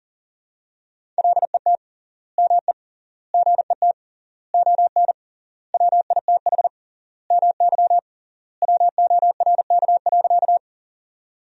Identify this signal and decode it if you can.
Morse code, heard as a keyed tone, about 37 wpm, 700 Hz. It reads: LET ME GET ON WITH MY WORK.
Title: 3x Word-Spacing - Sentences from Top 300 Words - 40wpm